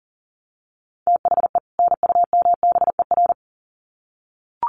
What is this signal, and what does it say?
Morse code, heard as a keyed tone, about 40 wpm, 700 Hz. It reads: THE NUMBER
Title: Sets of 2 Words 40wpm